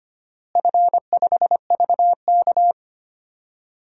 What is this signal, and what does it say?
Morse code, heard as a keyed tone, about 25 wpm, 700 Hz. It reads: F5VK
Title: DXCC-100 Call Signs - 4 Characters - Repeated 2x - 25wpm